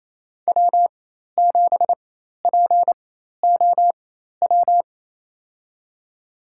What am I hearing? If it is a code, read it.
Morse code, heard as a keyed tone, about 28 wpm, spaced as for 13 wpm, 700 Hz. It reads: W7POW
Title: USA Call Signs - 5 Characters - 4x Character Spacing - 28wpm